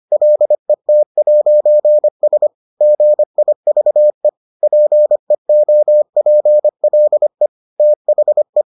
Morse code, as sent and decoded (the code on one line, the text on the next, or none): .-.. - .----. ... / --. .. ...- . / .--. --- .--. .-.. / - .... .
LET'S GIVE PEOPLE THE